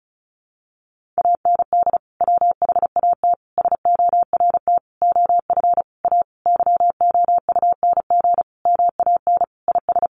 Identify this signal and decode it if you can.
Morse code, heard as a keyed tone, about 34 wpm, 700 Hz. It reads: AND WHAT SORT OF A YOUNG MAN IS
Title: Sentences from Top 1300 Words 35wpm